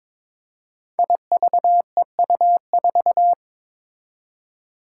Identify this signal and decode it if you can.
Morse code, heard as a keyed tone, about 22 wpm, 700 Hz. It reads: IVEU4
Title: Five Challenging Letter-Number Combinations 22wpm